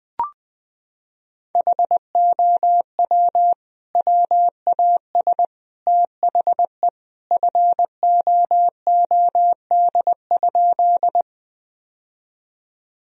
Morse code, heard as a keyed tone, about 20 wpm, 700 Hz. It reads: HOW WAS THE FOOD?
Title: Sentences from Top 1000 Words - Review Part 1 - 20wpm